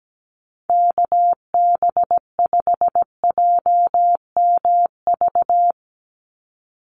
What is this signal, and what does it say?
Morse code, heard as a keyed tone, about 17 wpm, 700 Hz. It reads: KB5JMV
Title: USA Call Signs 17wpm